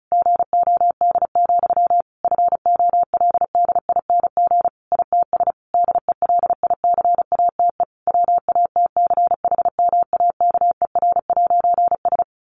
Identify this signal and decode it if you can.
Morse code, heard as a keyed tone, about 34 wpm, 700 Hz. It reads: GOD, FOLDING ITS DELICATE WATCHMAKER'S
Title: A Sound Of Thunder 35wpm